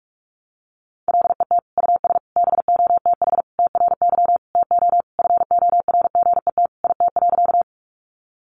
Morse code, heard as a keyed tone, about 44 wpm, 700 Hz. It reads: LET US BOTH TRY TO FORGET IT.
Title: Sentences from Top 1200 Words 45wpm